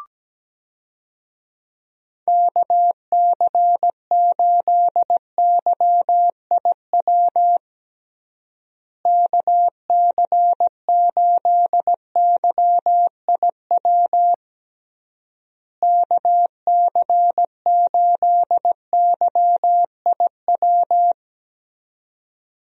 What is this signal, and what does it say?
Morse code, heard as a keyed tone, about 17 wpm, 700 Hz. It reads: KC8YIW KC8YIW KC8YIW
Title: USA Call Signs - 6 Characters - Repeated 3x - 17wpm